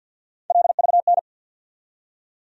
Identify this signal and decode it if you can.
Morse code, heard as a keyed tone, about 49 wpm, 700 Hz. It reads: LVN
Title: USA Call Sign Postfixes - 3 Letters 50wpm